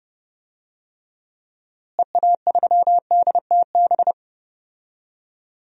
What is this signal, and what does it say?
Morse code, heard as a keyed tone, about 30 wpm, 700 Hz. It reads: EA3DTB
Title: DXCC-100 Call Signs - 6 Characters - Repeated 2x - 30wpm